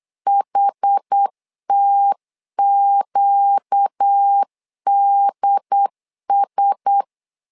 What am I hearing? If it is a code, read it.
Morse code, heard as a keyed tone, about 8 wpm, 800 Hz. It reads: HTQDS
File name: LEZIONI / 4 Q1S3 8wpm 3037